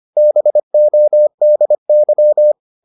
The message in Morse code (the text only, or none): BODY